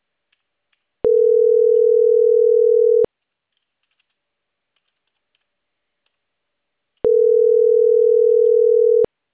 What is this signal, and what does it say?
Ringback tone (North America), 440 + 480 Hz.
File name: CENTRAL CAEA / out agcaea01 - .52595